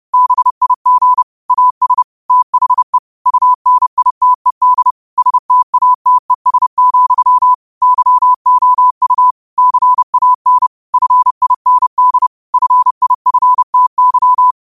Morse code, heard as a keyed tone, about 30 wpm, 1000 Hz. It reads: BIG AS THE UNITED STATES, YOU CAN FIND FIFTY